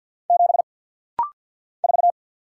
Morse code, heard as a keyed tone, about 49 wpm, 700 Hz. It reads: 7 4
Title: Single Letter-Number - Rapid-Fire 50wpm